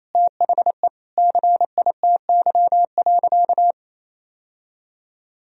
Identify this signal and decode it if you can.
Morse code, heard as a keyed tone, about 28 wpm, 690 Hz. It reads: THE CITY.
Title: Sentences from Top 1000 Words - Review Part 2 - 28wpm